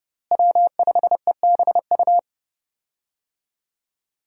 Morse code, heard as a keyed tone, about 30 wpm, 700 Hz. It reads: W5EBU
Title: USA Call Signs - 5 Characters - Repeated 2x - 30wpm